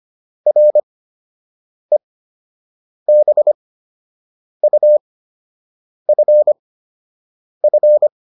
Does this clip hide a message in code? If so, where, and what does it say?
Yes: Morse code from 0.5 to 8.1 s, heard as a keyed tone, about 25 wpm, spaced as for 7 wpm, 600 Hz. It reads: REBUFF